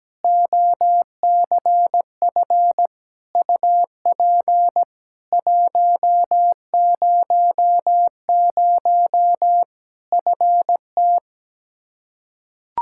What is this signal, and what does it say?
Morse code, heard as a keyed tone, about 17 wpm, 700 Hz. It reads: OCF UP 100 FT
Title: Practice QSO - Exchange 2 - Info - 17wpm